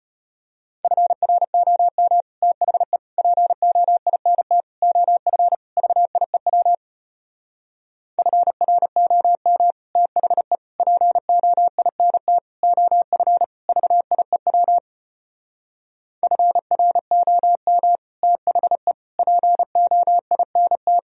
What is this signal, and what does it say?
Morse code, heard as a keyed tone, about 33 wpm, 700 Hz. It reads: FROM THE POINT OF VIEW FROM THE POINT OF VIEW FROM THE POINT